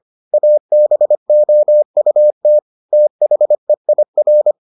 Morse code, heard as a keyed tone, about 25 wpm, 600 Hz. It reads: ABOUT THEIR